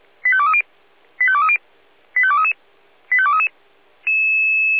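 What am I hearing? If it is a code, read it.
Five-tone selective call (ZVEI), heard as quick runs of five tones, tones of about 70 ms. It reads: 85230 85230 85233 85233
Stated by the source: ZVEI five-tone selective call received by SDR.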